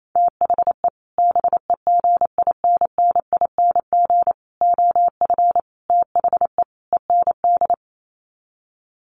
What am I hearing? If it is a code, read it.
Morse code, heard as a keyed tone, about 28 wpm, 700 Hz. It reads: THE BEGINNING OF THE END
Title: Sets of 5 Words Encore2 28wpm